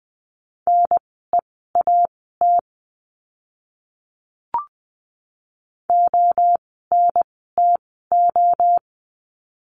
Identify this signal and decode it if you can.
Morse code, heard as a keyed tone, about 20 wpm, spaced as for 14 wpm, 700 Hz. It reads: NEAT ONTO
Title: 024 Lesson 5f - Letter o - Words 20wpm